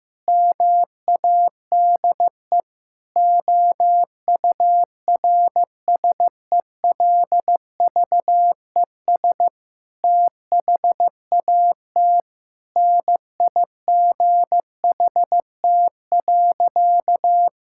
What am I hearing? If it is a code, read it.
Morse code, heard as a keyed tone, about 16 wpm, 700 Hz. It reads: MADE OURSELVES THAT NIGHT.
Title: Sentences from Top 1600 Words 15wpm